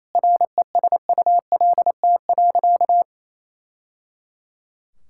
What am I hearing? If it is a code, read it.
Morse code, heard as a keyed tone, about 28 wpm, 700 Hz. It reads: RESULT.
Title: Sentences from Top 1000 Words - Review Part 5 - 28wpm